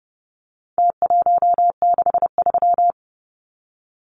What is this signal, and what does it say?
Morse code, heard as a keyed tone, about 30 wpm, 700 Hz. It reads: T163
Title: Numbers - Contest Style - 4-Digit 30wpm